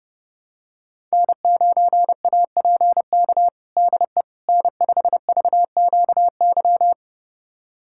Morse code, heard as a keyed tone, about 32 wpm, 700 Hz. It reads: N9APK DE N5VQY
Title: USA Callsign DE Callsign - 5 Characters - 30wpm